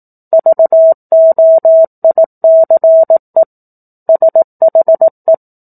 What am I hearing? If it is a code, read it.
Morse code, heard as a keyed tone, about 18 wpm, 650 Hz. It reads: VOICE SHE
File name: 04-Chapter-04 18 WPM